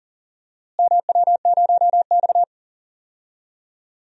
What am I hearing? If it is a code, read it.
Morse code, heard as a keyed tone, about 40 wpm, 700 Hz. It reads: MW0X
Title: DXCC-100 Call Signs - 4 Characters - Repeated 2x - 40wpm